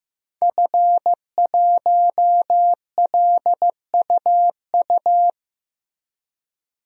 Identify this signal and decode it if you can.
Morse code, heard as a keyed tone, about 15 wpm, 700 Hz. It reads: F1LUU
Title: DXCC-100 Call Signs - 5 Characters - 15wpm